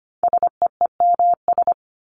Morse code, heard as a keyed tone, about 25 wpm, 700 Hz. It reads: SEEMS
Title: ES:S Suffix - Top 100 Words with Base Words 25wpm